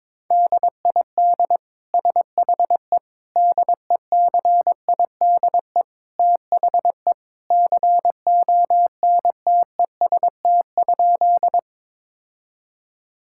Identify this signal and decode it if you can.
Morse code, heard as a keyed tone, about 21 wpm, 700 Hz. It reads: DID SHE DECIDE THE CONTEST?
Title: Sentences from Top 1600 Words 22wpm